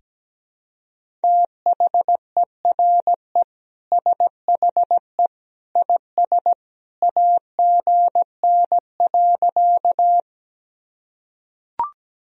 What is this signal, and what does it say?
Morse code, heard as a keyed tone, about 17 wpm, 700 Hz. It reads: THERE SHE IS AGN.
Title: Sentences from Top 200 Words and Abbreviations 17wpm